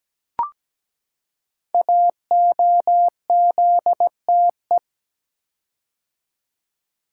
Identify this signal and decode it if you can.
Morse code, heard as a keyed tone, about 17 wpm, 700 Hz. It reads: AOZTE
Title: Code Groups - 5 Letters - 17wpm